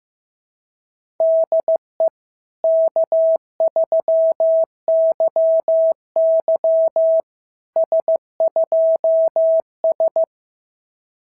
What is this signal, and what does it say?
Morse code, heard as a keyed tone, about 14 wpm, 650 Hz. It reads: DE K3YY S2S
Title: SOTA QSO Practice 15wpm